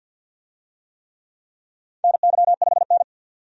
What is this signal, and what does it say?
Morse code, heard as a keyed tone, about 52 wpm, 700 Hz. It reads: NY5N
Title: USA Call Signs - 4 Characters - 50wpm